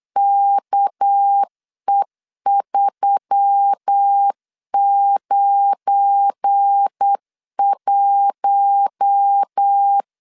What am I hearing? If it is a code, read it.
Morse code, heard as a keyed tone, about 8 wpm, 800 Hz. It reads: KE391
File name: LEZIONI / 7 GF9Z / 7 GF9 8wpm 3545